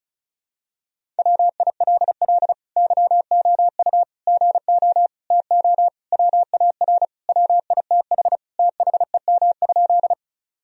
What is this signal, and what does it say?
Morse code, heard as a keyed tone, about 34 wpm, 700 Hz. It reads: WILL YOU GO TO WAR WITH THEM?
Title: Sentences from Top 1000 Words - Review Part 2 - 35wpm